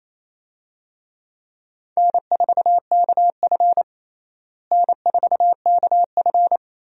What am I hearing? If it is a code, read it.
Morse code, heard as a keyed tone, about 28 wpm, 700 Hz. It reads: N4KF N4KF
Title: USA Call Signs - 4 Characters - Repeated 2x - 28wpm